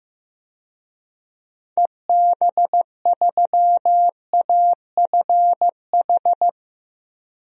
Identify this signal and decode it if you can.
Morse code, heard as a keyed tone, about 15 wpm, 700 Hz. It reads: EB3AFH
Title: DXCC-100 Call Signs - 6 Characters - Repeated 2x - 15wpm